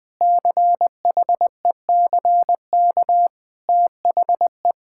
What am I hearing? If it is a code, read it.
Morse code, heard as a keyed tone, about 21 wpm, 700 Hz. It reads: CHECK THE
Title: Sentences with 6-Word Phrases - 20wpm